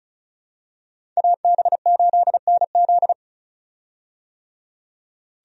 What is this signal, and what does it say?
Morse code, heard as a keyed tone, about 34 wpm, 700 Hz. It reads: AB8NZ
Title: USA Call Signs - 5 Characters - 35wpm